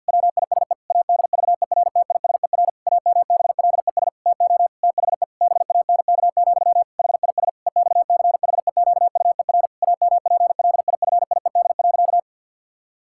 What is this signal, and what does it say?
Morse code, heard as a keyed tone, about 50 wpm, 700 Hz. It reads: WISE ADVERTISER AMBLES TO THE BANK, HIS EXCHEQUER AMPLIFIED.
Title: Pangram Challenge v2 - 50wpm